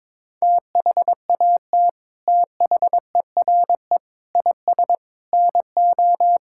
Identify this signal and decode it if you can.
Morse code, heard as a keyed tone, about 22 wpm, 700 Hz. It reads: THAT THERE IS NO
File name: practice / 22wpm / Sets of 8 Words - 22wpm